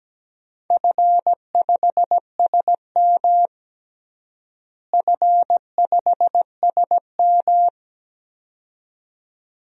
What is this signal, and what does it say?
Morse code, heard as a keyed tone, about 17 wpm, 700 Hz. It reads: F5SM F5SM
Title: DXCC-100 Call Signs - 4 Characters - Repeated 2x - 17wpm